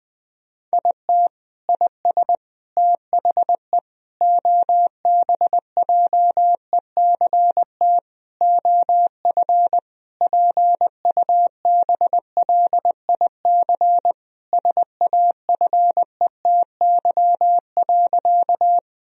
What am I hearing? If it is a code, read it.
Morse code, heard as a keyed tone, about 20 wpm, 700 Hz. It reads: IT IS THE OBJECT OF PUBLIC SAFETY.
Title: Sentences from Top 1000 Words - Review Part 4 - 20wpm